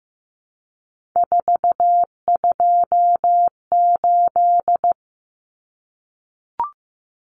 Morse code, heard as a keyed tone, about 15 wpm, 700 Hz. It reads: 428
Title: Common Cut Numbers - 3-Digit 15wpm